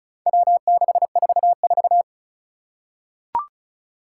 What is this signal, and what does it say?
Morse code, heard as a keyed tone, about 36 wpm, 700 Hz. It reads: W644
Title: Four Challenging Letter-Number Combinations 35wpm